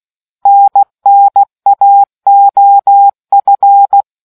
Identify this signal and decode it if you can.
Morse code, heard as a keyed tone, about 16 wpm, 800 Hz. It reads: NNAOF